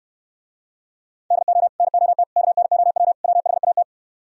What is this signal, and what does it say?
Morse code, heard as a keyed tone, about 71 wpm, 700 Hz. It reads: DO NOT KNOW WHAT